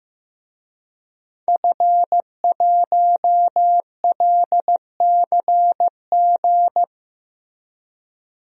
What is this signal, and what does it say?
Morse code, heard as a keyed tone, about 15 wpm, 700 Hz. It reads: F1LCG
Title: DXCC-100 Call Signs - 5 Characters - Repeated 3x - 15wpm